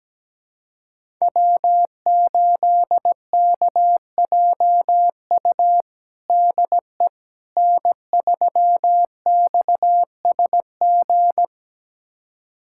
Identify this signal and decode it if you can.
Morse code, heard as a keyed tone, about 17 wpm, 700 Hz. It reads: W8KJU DE N3XSG
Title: USA Callsign DE Callsign - 5 Characters - 17wpm